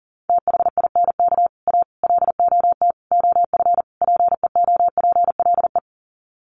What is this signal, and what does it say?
Morse code, heard as a keyed tone, about 40 wpm, 700 Hz. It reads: THINK A LOT OF PEOPLE
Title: Sets of 5 Words 40wpm